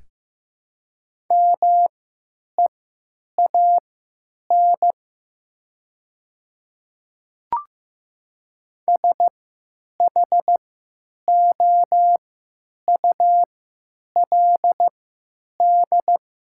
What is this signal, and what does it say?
Morse code, heard as a keyed tone, about 15 wpm, spaced as for 8 wpm, 700 Hz. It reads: MEAN SHOULD
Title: 3x Character Spacing - Top 500 Words - 15wpm